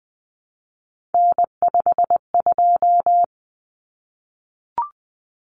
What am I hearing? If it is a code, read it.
Morse code, heard as a keyed tone, about 20 wpm, 700 Hz. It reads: N52
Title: Common Cut Numbers - 3-Digit 20wpm